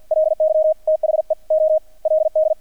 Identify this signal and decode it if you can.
Morse code, heard as a keyed tone, about 50 wpm, spaced as for 40 wpm, 630 Hz, with radio noise. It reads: LY THE OLD
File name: archived / CRSnet